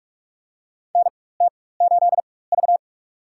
Morse code, heard as a keyed tone, about 44 wpm, spaced as for 21 wpm, 700 Hz. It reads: NT8V
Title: USA Call Signs - 4 Characters - 4x Character Spacing - 45wpm